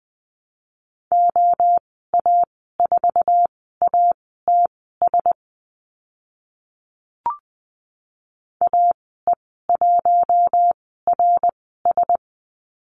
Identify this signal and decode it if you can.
Morse code, heard as a keyed tone, about 20 wpm, spaced as for 14 wpm, 700 Hz. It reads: OA4ATS AE1RS
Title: 055 Lesson 10g - Letter r - Call Signs 20wpm